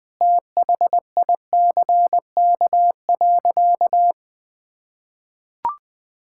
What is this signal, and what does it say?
Morse code, heard as a keyed tone, about 20 wpm, 700 Hz. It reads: THICK.
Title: Sentences from Top 1600 Words 20wpm